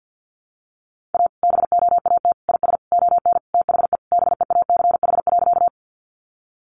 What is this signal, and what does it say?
Morse code, heard as a keyed tone, about 50 wpm, 700 Hz. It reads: A BOAT IS ON THE BEACH.
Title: Sentences from Top 1500 Words 50wpm